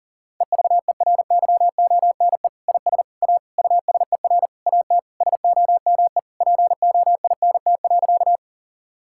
Morse code, heard as a keyed tone, about 40 wpm, 700 Hz. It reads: EVERYONE IS A USER AT SOME POINT.
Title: Sentences from Top 1000 Words - Review Part 5 - 40wpm